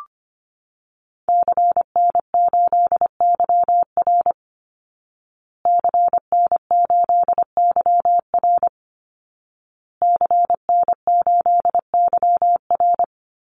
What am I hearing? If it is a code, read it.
Morse code, heard as a keyed tone, about 25 wpm, 700 Hz. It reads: CN8YR CN8YR CN8YR
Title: Call Signs Repeated 2x - 25wpm